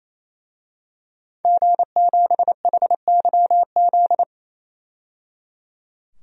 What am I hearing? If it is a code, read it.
Morse code, heard as a keyed tone, about 28 wpm, 700 Hz. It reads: G7HYZ
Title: DXCC-100 Call Signs - 5 Characters - 28wpm